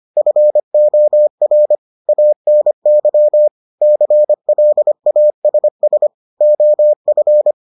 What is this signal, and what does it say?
Morse code, heard as a keyed tone, about 25 wpm, 600 Hz. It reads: FOR ANY CLASS OF